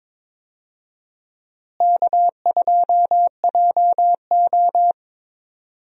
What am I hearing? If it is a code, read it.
Morse code, heard as a keyed tone, about 22 wpm, 700 Hz. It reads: K2JO
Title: USA Call Signs - 4 Characters - Repeated 2x - 22wpm